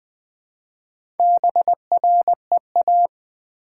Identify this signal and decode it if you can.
Morse code, heard as a keyed tone, about 20 wpm, 700 Hz. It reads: BREA
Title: Build A Word - 1 Letter At A Time - 3 to 5-Letter Words - 20wpm